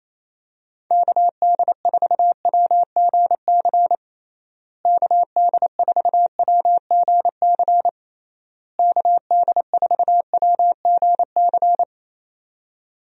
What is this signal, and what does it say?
Morse code, heard as a keyed tone, about 27 wpm, 700 Hz. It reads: KD4WGC KD4WGC KD4WGC
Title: USA Call Signs - 6 Characters - Repeated 3x - 28wpm